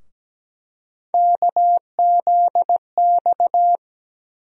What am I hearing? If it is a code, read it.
Morse code, heard as a keyed tone, about 17 wpm, 700 Hz. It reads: KZX
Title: USA Call Sign Postfixes - 3 Letters 17wpm